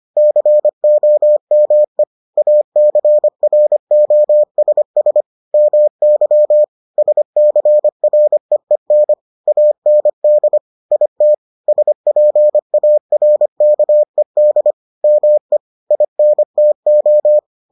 Morse code, heard as a keyed tone, about 25 wpm, 600 Hz. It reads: COME ACROSS MY SCREEN AND IT SPARKED ME INTO